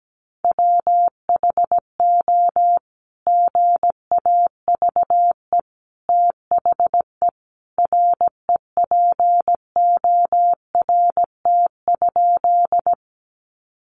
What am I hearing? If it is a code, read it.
Morse code, heard as a keyed tone, about 17 wpm, 700 Hz. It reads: WHO GAVE THE REPORT?
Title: Sentences from Top 900 Words 17wpm